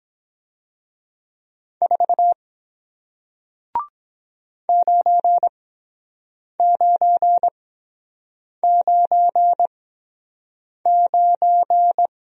Morse code, heard as a keyed tone, about 23 wpm, spaced as for 7 wpm, 700 Hz. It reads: 4 9999